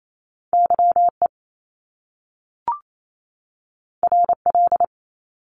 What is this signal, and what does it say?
Morse code, heard as a keyed tone, about 28 wpm, 700 Hz. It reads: Y E R L
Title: Top Two Letter Combinations in Words 28wpm